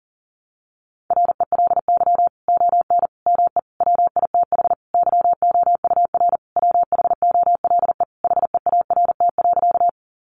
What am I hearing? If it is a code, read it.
Morse code, heard as a keyed tone, about 40 wpm, 700 Hz. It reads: RELY ON ME WITH YOUR WHOLE HEART.
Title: Sentences from Top 1200 Words 40wpm